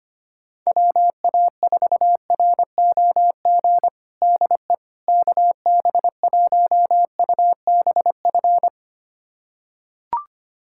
Morse code, heard as a keyed tone, about 25 wpm, 700 Hz. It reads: WA4ROG DE KB1UBF
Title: USA Callsign DE Callsign - 6 Characters - 25wpm